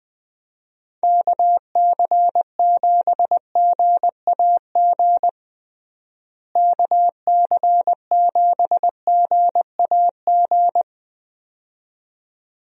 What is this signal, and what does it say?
Morse code, heard as a keyed tone, about 21 wpm, 700 Hz. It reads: KC7GAG KC7GAG
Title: USA Call Signs - 6 Characters - Repeated 2x - 20wpm